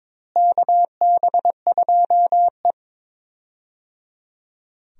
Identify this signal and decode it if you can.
Morse code, heard as a keyed tone, about 22 wpm, 700 Hz. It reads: KB2E
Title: USA Call Signs - 4 Characters - Repeated 3x - 22wpm